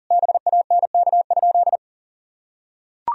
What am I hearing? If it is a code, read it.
Morse code, heard as a keyed tone, about 40 wpm, 700 Hz. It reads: BANK?
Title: Sentences from Top 1000 Words - Review Part 4 - 40wpm